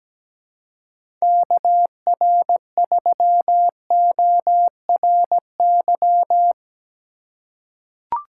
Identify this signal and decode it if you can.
Morse code, heard as a keyed tone, about 17 wpm, 700 Hz. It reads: KR3ORY
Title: USA Call Signs - 6 Characters - 4x Character Spacing - 17wpm